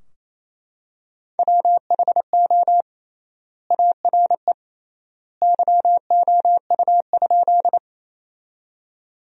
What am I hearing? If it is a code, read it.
Morse code, heard as a keyed tone, about 28 wpm, 700 Hz. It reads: WHO ARE YOU?
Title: 3x Word-Spacing - Sentences from Top 100 Words - 28wpm